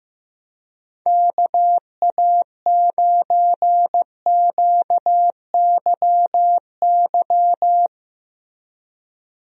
Morse code, heard as a keyed tone, about 15 wpm, 700 Hz. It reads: KA9QYY
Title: USA Call Signs - 6 Characters - Repeated 2x - 15wpm